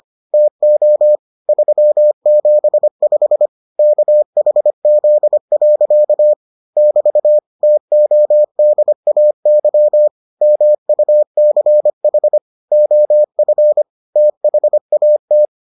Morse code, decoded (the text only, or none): TO 375 KHZ. <BT> TODAY MUCH OF THAT